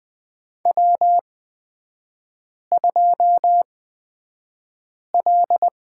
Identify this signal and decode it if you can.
Morse code, heard as a keyed tone, about 20 wpm, spaced as for 5 wpm, 700 Hz. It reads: W2L
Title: 236 Lesson 38e - Letter q - Three Characters 20wpm